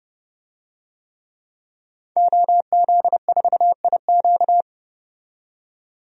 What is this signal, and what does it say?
Morse code, heard as a keyed tone, about 30 wpm, 700 Hz. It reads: OZ4IQ
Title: DXCC-100 Call Signs - 5 Characters - 30wpm